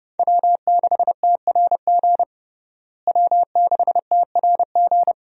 Morse code, heard as a keyed tone, about 30 wpm, 700 Hz. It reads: W6TRG W6TRG
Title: USA Call Signs - 5 Characters - Repeated 2x - 30wpm